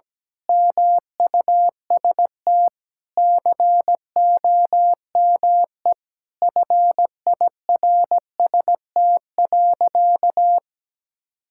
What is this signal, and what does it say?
Morse code, heard as a keyed tone, about 17 wpm, 700 Hz. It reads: MUST COME FIRST.
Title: Sentences from Top 1000 Words - Review Part 6 - 17wpm